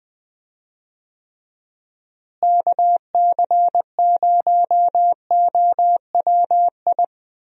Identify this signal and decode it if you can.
Morse code, heard as a keyed tone, about 20 wpm, 700 Hz. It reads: KC0OWI
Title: USA Call Signs - 6 Characters - Repeated 3x - 20wpm